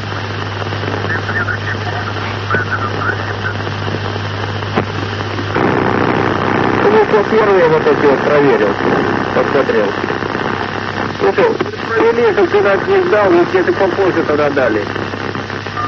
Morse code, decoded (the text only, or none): none